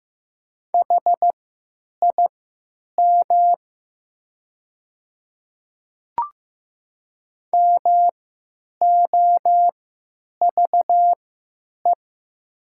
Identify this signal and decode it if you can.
Morse code, heard as a keyed tone, about 15 wpm, spaced as for 9 wpm, 700 Hz. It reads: HIM MOVE
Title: 3x Character Spacing - Top 200 Words - 15wpm